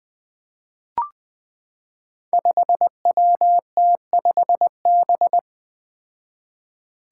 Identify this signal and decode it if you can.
Morse code, heard as a keyed tone, about 20 wpm, 700 Hz. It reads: 5WT5B
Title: Five Challenging Letter-Number Combinations 20wpm